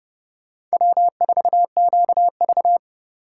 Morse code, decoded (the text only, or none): W4QV